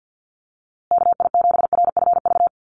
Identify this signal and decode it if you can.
Morse code, heard as a keyed tone, about 50 wpm, 700 Hz. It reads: KI7RFV
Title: USA Call Signs Repeated 2x 50wpm